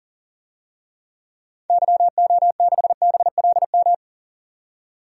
Morse code, heard as a keyed tone, about 40 wpm, 700 Hz. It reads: YO6BLM
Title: DXCC-100 Call Signs - 6 Characters - 40wpm